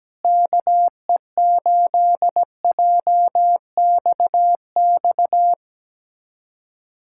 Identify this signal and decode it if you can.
Morse code, heard as a keyed tone, about 17 wpm, 700 Hz. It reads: KE8JXX
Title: USA Callsign DE Callsign - 6 Characters - 17wpm